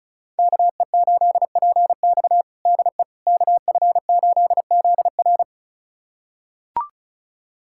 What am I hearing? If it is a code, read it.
Morse code, heard as a keyed tone, about 34 wpm, 700 Hz. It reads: KE8PX DE KF8ZR
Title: USA Callsign DE Callsign - 5 Characters - 35wpm